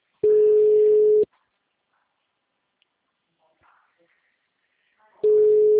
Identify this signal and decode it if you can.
Ringback tone (Europe, 425 Hz).